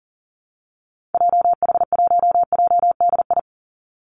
Morse code, heard as a keyed tone, about 40 wpm, 700 Hz. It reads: JH1JDI
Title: Call Signs - 40wpm